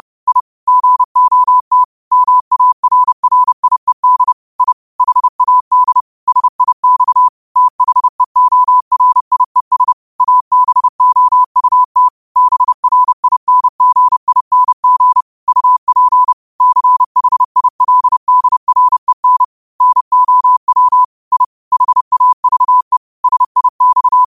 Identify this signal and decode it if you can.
Morse code, heard as a keyed tone, about 30 wpm, 1000 Hz. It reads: I GOT MARRIED I HAD SIX THEORIES ABOUT BRINGING UP CHILDREN NOW I HAVE SIX
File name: qotdfiles / 30wpm